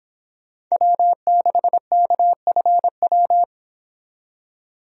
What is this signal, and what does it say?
Morse code, heard as a keyed tone, about 26 wpm, 700 Hz. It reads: W6KFW